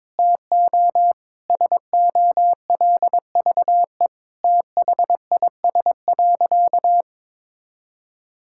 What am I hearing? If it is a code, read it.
Morse code, heard as a keyed tone, about 21 wpm, 700 Hz. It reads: TO SOLVE THIS.
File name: practice / 22wpm / Sentences with 4-Word Phrases - 22wpm